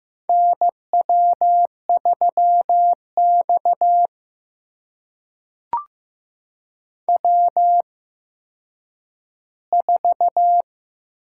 Morse code, heard as a keyed tone, about 15 wpm, spaced as for 9 wpm, 700 Hz. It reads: NW3X W4
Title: USA Call Signs - 4 Characters - 8x Character Spacing - 15wpm